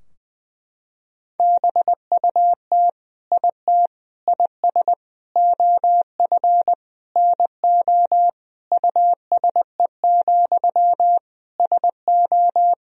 Morse code, heard as a keyed tone, about 20 wpm, 700 Hz. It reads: BUT IT IS OF NO USE, SO
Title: Sentences from Top 1000 Words - Review Part 2 - 20wpm